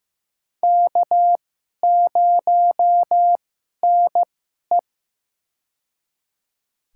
Morse code, heard as a keyed tone, about 15 wpm, spaced as for 11 wpm, 700 Hz. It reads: K0NE
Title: USA Call Signs - 2x Character Spacing - 15wpm